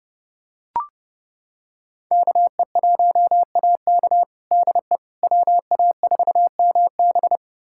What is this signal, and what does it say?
Morse code, heard as a keyed tone, about 30 wpm, 700 Hz. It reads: KE1AK DE WA4MB
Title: USA Callsign DE Callsign - 5 Characters - 30wpm